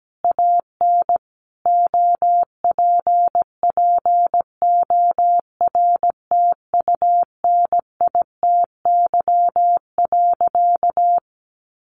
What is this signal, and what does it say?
Morse code, heard as a keyed tone, about 17 wpm, 700 Hz. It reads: AN OPPORTUNITY.
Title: Sentences from Top 1000 Words 17wpm